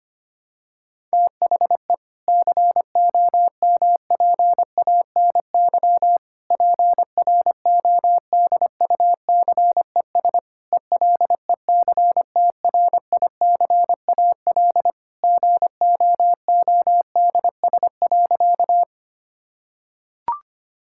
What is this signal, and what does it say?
Morse code, heard as a keyed tone, about 25 wpm, 700 Hz. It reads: THE COMPANY PRODUCES ELECTRICAL GOODS.
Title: Sentences from Top 1000 Words - Review Part 6 - 25wpm